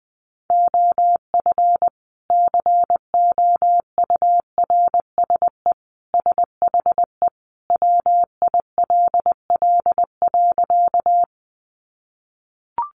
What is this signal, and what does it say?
Morse code, heard as a keyed tone, about 20 wpm, 700 Hz. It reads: OF COURSE SHE WILL.
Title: Sentences from Top 500 Words 20wpm